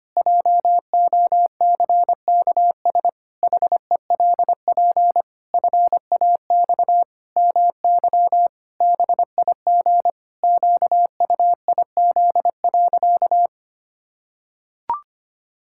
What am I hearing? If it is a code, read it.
Morse code, heard as a keyed tone, about 25 wpm, 700 Hz. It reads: JOCKS HELP FAX MY BIG QUIZ.
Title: Pangram Challenge v2 - 25wpm